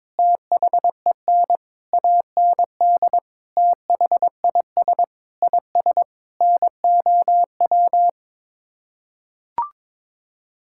Morse code, heard as a keyed tone, about 22 wpm, 700 Hz. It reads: THEN AND THIS IS NOW
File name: practice / 22wpm / Sets of 8 Words - 22wpm